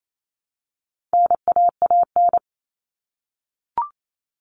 Morse code, heard as a keyed tone, about 28 wpm, 700 Hz. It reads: NAAN
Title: All Cut Numbers - 4-Digit 28wpm